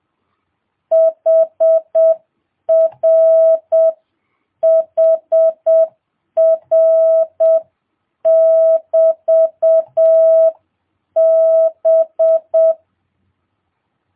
Morse code, heard as a keyed tone, about 7 wpm, 650 Hz. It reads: HR HR <BT> B
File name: jh8cbh / 2022cw